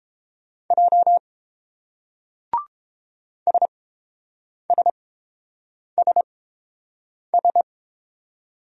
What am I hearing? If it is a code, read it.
Morse code, heard as a keyed tone, about 30 wpm, spaced as for 7 wpm, 700 Hz. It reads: J SSSS